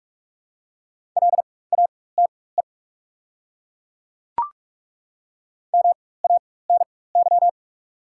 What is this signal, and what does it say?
Morse code, heard as a keyed tone, about 43 wpm, spaced as for 21 wpm, 700 Hz. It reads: LATE MANY
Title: 4x Character Spacing - Top 300 Words - 45wpm